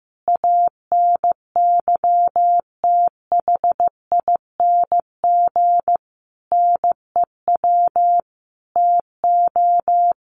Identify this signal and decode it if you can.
Morse code, heard as a keyed tone, about 15 wpm, 700 Hz. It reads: ANYTHING NEW TO
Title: Sentences from Top 1300 Words 15wpm